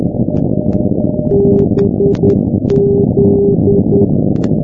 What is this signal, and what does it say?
Morse code, heard as a keyed tone, about 10 wpm, 400 Hz, with noise. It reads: BZ
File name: f-2403-bgp-387+g-205-bz-386-sdr-hu-g-south west uk